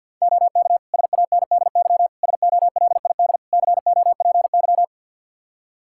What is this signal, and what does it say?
Morse code, heard as a keyed tone, about 52 wpm, 700 Hz. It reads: OK SANDY SOLID COPY
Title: Practice QSO - Exchange 3 - More Info - 50wpm